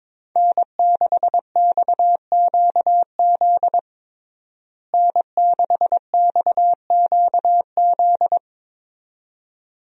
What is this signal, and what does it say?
Morse code, heard as a keyed tone, about 22 wpm, 700 Hz. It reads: N6XQZ N6XQZ
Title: USA Call Signs - 5 Characters - Repeated 2x - 22wpm